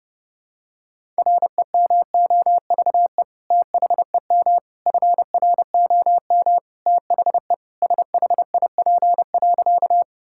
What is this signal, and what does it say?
Morse code, heard as a keyed tone, about 32 wpm, 700 Hz. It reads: REMOVE THEM FROM THE SHIP.
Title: Sentences from Top 1000 Words - Review Part 5 - 30wpm